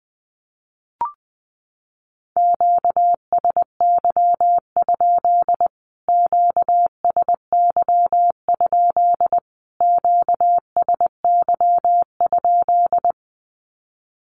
Morse code, heard as a keyed tone, about 20 wpm, 700 Hz. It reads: QSY? QSY? QSY?
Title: Q-Codes Repeated 3x 20wpm